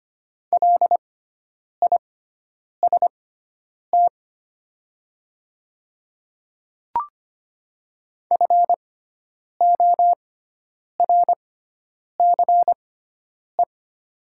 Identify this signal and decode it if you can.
Morse code, heard as a keyed tone, about 25 wpm, spaced as for 9 wpm, 700 Hz. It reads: LIST FORCE
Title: 6x Character Spacing - Top 500 Words - 25wpm